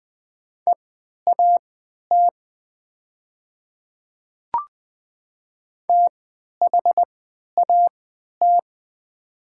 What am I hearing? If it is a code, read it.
Morse code, heard as a keyed tone, about 20 wpm, spaced as for 11 wpm, 700 Hz. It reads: EAT THAT
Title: 3x Character Spacing - Top 300 Words - 20wpm